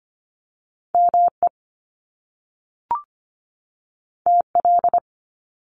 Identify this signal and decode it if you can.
Morse code, heard as a keyed tone, about 25 wpm, 700 Hz. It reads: ME TL
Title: Top Two Letter Combinations in Words 25wpm